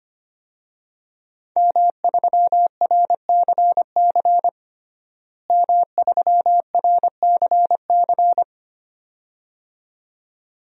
Morse code, heard as a keyed tone, about 25 wpm, 700 Hz. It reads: M3RCC M3RCC
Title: DXCC-100 Call Signs - 5 Characters - Repeated 2x - 25wpm